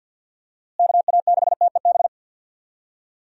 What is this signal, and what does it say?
Morse code, heard as a keyed tone, about 52 wpm, 700 Hz. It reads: KA6TEB
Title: USA Call Signs - 6 Characters - Repeated 3x - 50wpm